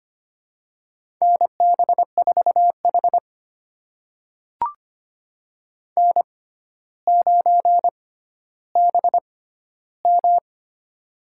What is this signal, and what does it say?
Morse code, heard as a keyed tone, about 25 wpm, spaced as for 13 wpm, 700 Hz. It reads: NB4H N9BM
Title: USA Call Signs - 4 Characters - 6x Character Spacing - 25wpm